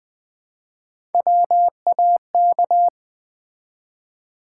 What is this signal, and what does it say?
Morse code, heard as a keyed tone, about 20 wpm, 700 Hz. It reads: WAK
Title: Build A Word - 1 Letter At A Time - 3 to 5-Letter Words - 20wpm